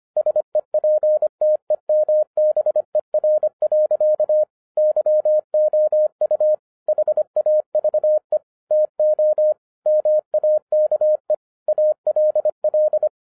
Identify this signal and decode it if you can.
Morse code, heard as a keyed tone, about 25 wpm, 610 Hz. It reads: SEPTEMBER. YOU HAVE TO MAKE ALL